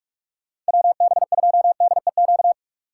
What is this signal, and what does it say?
Morse code, heard as a keyed tone, about 65 wpm, spaced as for 55 wpm, 700 Hz. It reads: WB2DEQ
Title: Speed Racing - USA 2x3 Call Signs 45wpm